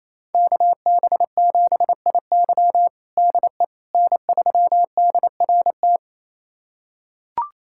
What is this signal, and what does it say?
Morse code, heard as a keyed tone, about 28 wpm, 700 Hz. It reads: KB7IY DE N3DRT
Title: USA Callsign DE Callsign - 5 Characters - 28wpm